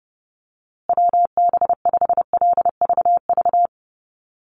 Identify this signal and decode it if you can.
Morse code, heard as a keyed tone, about 30 wpm, 700 Hz. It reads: WB5LVV
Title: USA Call Signs Repeated 2x 30wpm